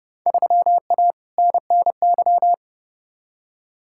Morse code, heard as a keyed tone, about 32 wpm, 700 Hz. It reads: 3A NNY
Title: Field Day Exchanges - 30wpm